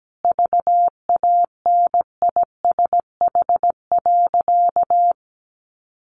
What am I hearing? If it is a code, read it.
Morse code, heard as a keyed tone, about 17 wpm, 700 Hz. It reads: VANISH.
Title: A Sound Of Thunder 17wpm